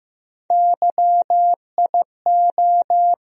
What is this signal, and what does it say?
Morse code, heard as a keyed tone, about 15 wpm, 700 Hz. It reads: YIO
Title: USA Call Sign Postfixes - 3 Letters 15wpm